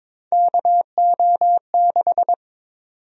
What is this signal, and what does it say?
Morse code, heard as a keyed tone, about 22 wpm, 700 Hz. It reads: KO6
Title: USA Call Sign Prefixes - 22wpm